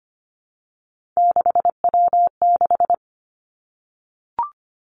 Morse code, heard as a keyed tone, about 25 wpm, 700 Hz. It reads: 6W6
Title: Three Challenging Letter-Number Combinations 25wpm